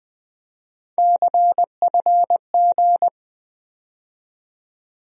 Morse code, heard as a keyed tone, about 20 wpm, 700 Hz. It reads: CFG
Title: USA Call Sign Postfixes - 3 Letters 20wpm